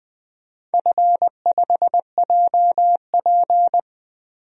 Morse code, heard as a keyed tone, about 20 wpm, 700 Hz. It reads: F5JP
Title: DXCC-100 Call Signs - 4 Characters - Repeated 2x - 20wpm